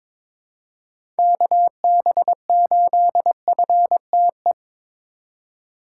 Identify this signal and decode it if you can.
Morse code, heard as a keyed tone, about 22 wpm, 700 Hz. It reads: KB8FTE
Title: USA Call Signs - 3x Character Spacing - 22wpm